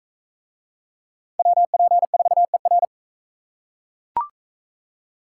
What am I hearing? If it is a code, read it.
Morse code, heard as a keyed tone, about 42 wpm, 700 Hz. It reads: WP4ER